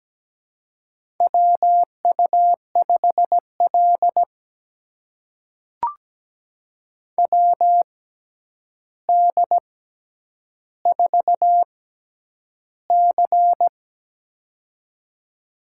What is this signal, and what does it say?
Morse code, heard as a keyed tone, about 17 wpm, spaced as for 9 wpm, 700 Hz. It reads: WU5L WD4C
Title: USA Call Signs - 4 Characters - 6x Character Spacing - 17wpm